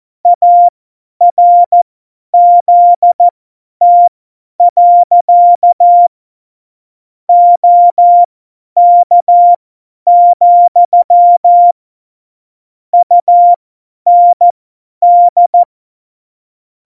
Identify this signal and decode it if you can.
Morse code, heard as a keyed tone, about 14 wpm, spaced as for 10 wpm, 700 Hz. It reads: ARZT. OK, UND
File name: Morsekurs / Witze / Witz 22. August 17, Tempo 50 F70